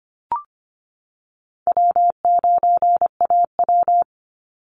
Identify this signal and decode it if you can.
Morse code, heard as a keyed tone, about 25 wpm, 700 Hz. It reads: W9AW
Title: Easy Call Signs Repeated 3x - 25wpm